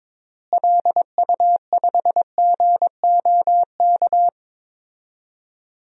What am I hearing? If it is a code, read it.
Morse code, heard as a keyed tone, about 22 wpm, 700 Hz. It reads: LU5GOK
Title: DXCC-100 Call Signs - 6 Characters - Repeated 3x - 22wpm